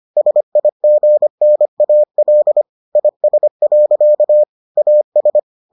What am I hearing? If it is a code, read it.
Morse code, heard as a keyed tone, about 25 wpm, 600 Hz. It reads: SIGNAL IS. AS